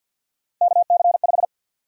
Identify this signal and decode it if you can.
Morse code, heard as a keyed tone, about 51 wpm, 700 Hz. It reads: KC5
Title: USA Call Sign Prefixes - 50wpm